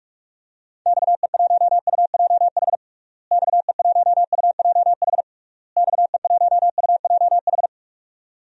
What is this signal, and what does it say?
Morse code, heard as a keyed tone, about 46 wpm, 700 Hz. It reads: XE1UJH XE1UJH XE1UJH
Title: DXCC-100 Call Signs - 6 Characters - Repeated 3x - 45wpm